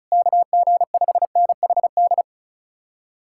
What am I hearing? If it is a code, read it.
Morse code, heard as a keyed tone, about 46 wpm, spaced as for 41 wpm, 700 Hz. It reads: KG5NHD
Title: Speed-Racing Common QSO Phrases Encore - 35wpm